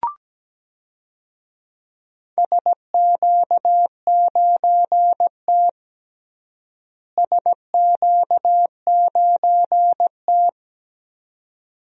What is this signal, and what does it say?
Morse code, heard as a keyed tone, about 17 wpm, 700 Hz. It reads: SQ9T SQ9T
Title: DXCC-100 Call Signs - 4 Characters - Repeated 2x - 17wpm